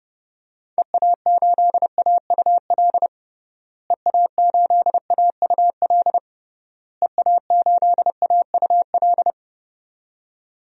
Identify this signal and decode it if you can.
Morse code, heard as a keyed tone, about 32 wpm, 700 Hz. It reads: EA8AUL EA8AUL EA8AUL
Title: DXCC-100 Call Signs - 6 Characters - Repeated 3x - 30wpm